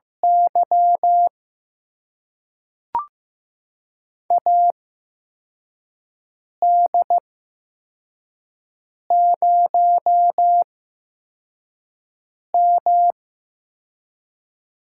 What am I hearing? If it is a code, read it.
Morse code, heard as a keyed tone, about 15 wpm, spaced as for 4 wpm, 700 Hz. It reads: Y AD0M
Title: USA Call Signs - 8x Character Spacing - 15wpm